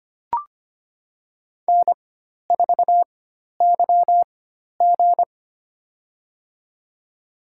Morse code, heard as a keyed tone, about 25 wpm, spaced as for 12 wpm, 700 Hz. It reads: N4YG
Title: USA Call Signs - 4x Character Spacing - 25wpm